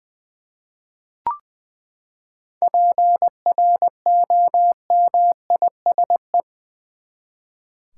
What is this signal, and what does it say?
Morse code, heard as a keyed tone, about 20 wpm, 700 Hz. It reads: PROMISE
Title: Seven Letter Words - 20wpm